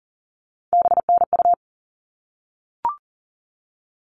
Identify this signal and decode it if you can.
Morse code, heard as a keyed tone, about 40 wpm, 700 Hz. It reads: BNU
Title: All Cut Numbers - 3-Digit 40wpm